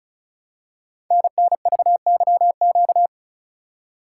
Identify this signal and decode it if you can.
Morse code, heard as a keyed tone, about 35 wpm, 700 Hz. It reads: NNVYQ
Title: Code Groups - 5 Letters - 35wpm